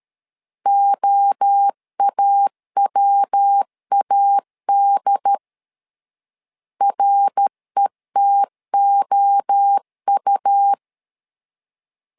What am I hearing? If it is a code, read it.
Morse code, heard as a keyed tone, about 13 wpm, 800 Hz. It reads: OAWAD RETOU
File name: LEZIONI / 3 MWRO / 3 MWRO 12wpm 1730